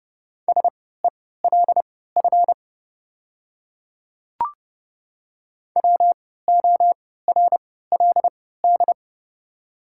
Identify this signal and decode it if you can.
Morse code, heard as a keyed tone, about 30 wpm, spaced as for 17 wpm, 690 Hz. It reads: SELF WORLD
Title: 3x Character Spacing - Top 200 Words - 30wpm